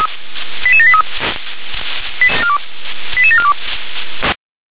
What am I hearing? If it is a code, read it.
Five-tone selective call (ZVEI), heard as quick runs of five tones, tones of about 70 ms. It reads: 88783 88742